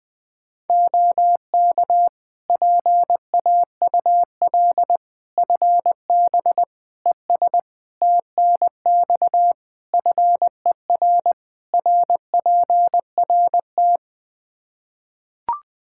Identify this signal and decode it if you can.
Morse code, heard as a keyed tone, about 20 wpm, 700 Hz. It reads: OK PAUL FB ES TNX FER RPRT
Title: K1USN SST QSO Practice 20wpm